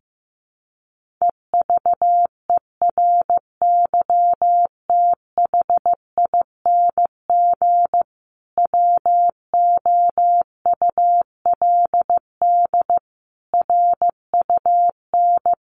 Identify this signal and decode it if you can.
Morse code, heard as a keyed tone, about 15 wpm, 700 Hz. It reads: EVERYTHING WOULD RUN
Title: Sentences from Top 1400 Words 15wpm